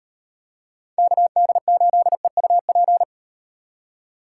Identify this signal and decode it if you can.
Morse code, heard as a keyed tone, about 36 wpm, 700 Hz. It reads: KD8EUP